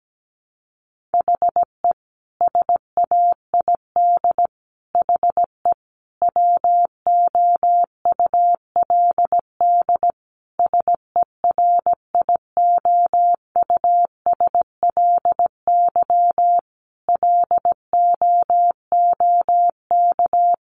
Morse code, heard as a keyed tone, about 17 wpm, 700 Hz. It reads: HE SAID HE WOULD SERIOUSLY LOOK
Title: Sentences from Top 1200 Words 17wpm